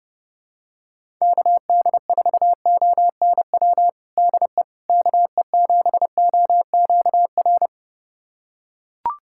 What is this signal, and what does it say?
Morse code, heard as a keyed tone, about 29 wpm, 700 Hz. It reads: KD4ONW DE KE7OQR
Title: USA Callsign DE Callsign - 6 Characters - 30wpm